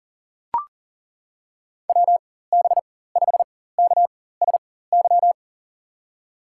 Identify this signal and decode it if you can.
Morse code, heard as a keyed tone, about 40 wpm, spaced as for 19 wpm, 700 Hz. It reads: WB5KSY
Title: USA Call Signs - 6 Characters - 4x Character Spacing - 40wpm